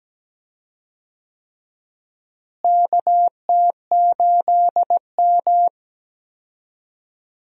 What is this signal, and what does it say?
Morse code, heard as a keyed tone, about 17 wpm, 700 Hz. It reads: KT8M
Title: USA Call Signs - 4 Characters - Repeated 3x - 17wpm